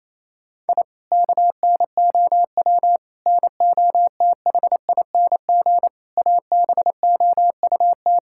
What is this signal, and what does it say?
Morse code, heard as a keyed tone, about 28 wpm, 690 Hz. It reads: I KNOW NOTHING ABOUT
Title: Sentences from Top 1000 Words - Review Part 6 - 28wpm